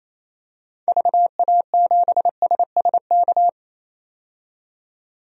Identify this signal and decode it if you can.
Morse code, heard as a keyed tone, about 28 wpm, 700 Hz. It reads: VA7SSK
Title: DXCC-100 Call Signs - 6 Characters - 28wpm